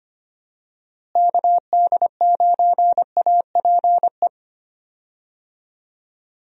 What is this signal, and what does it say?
Morse code, heard as a keyed tone, about 25 wpm, 700 Hz. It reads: KD9APE
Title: USA Call Signs - 6 Characters - Repeated 2x - 25wpm